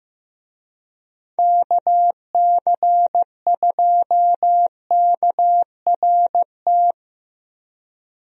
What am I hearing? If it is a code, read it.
Morse code, heard as a keyed tone, about 15 wpm, 700 Hz. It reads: KC2KRT
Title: USA Call Signs - 6 Characters - Repeated 2x - 15wpm